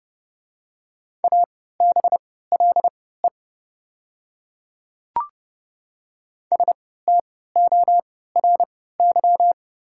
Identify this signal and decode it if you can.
Morse code, heard as a keyed tone, about 31 wpm, spaced as for 17 wpm, 700 Hz. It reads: ABLE STORY
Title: 3x Character Spacing - Top 500 Words - 30wpm